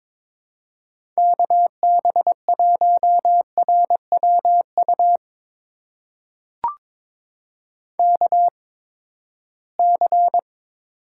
Morse code, heard as a keyed tone, about 22 wpm, spaced as for 15 wpm, 700 Hz. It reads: KB1RWU KC
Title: USA Call Signs - 6 Characters - 8x Character Spacing - 22wpm